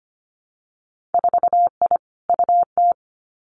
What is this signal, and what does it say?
Morse code, heard as a keyed tone, about 25 wpm, 700 Hz. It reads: 4I UT
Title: Winter Field Day Exchanges 25wpm